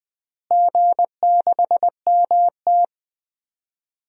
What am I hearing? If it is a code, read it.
Morse code, heard as a keyed tone, about 21 wpm, 700 Hz. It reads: G6MT
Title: DXCC-100 Call Signs - 4 Characters - Repeated 3x - 20wpm